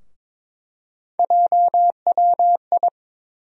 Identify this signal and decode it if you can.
Morse code, heard as a keyed tone, about 22 wpm, 700 Hz. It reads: JWI
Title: USA Call Sign Postfixes - 3 Letters 22wpm